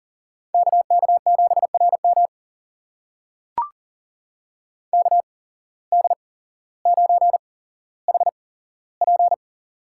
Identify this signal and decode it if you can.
Morse code, heard as a keyed tone, about 40 wpm, spaced as for 17 wpm, 700 Hz. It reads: KK7RM KD9HP
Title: USA Call Signs - 8x Character Spacing - 40wpm